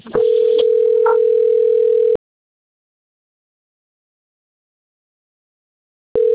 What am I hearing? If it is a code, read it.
Ringback tone (North America), 440 + 480 Hz.